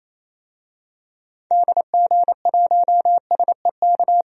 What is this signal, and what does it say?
Morse code, heard as a keyed tone, about 28 wpm, 700 Hz. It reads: DG1SEK
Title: DXCC-100 Call Signs - 6 Characters - 28wpm